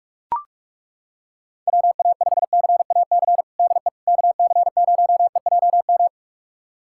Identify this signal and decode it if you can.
Morse code, heard as a keyed tone, about 41 wpm, 690 Hz. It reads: WA5CAC DE KC0EJM
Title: USA Callsign DE Callsign 45wpm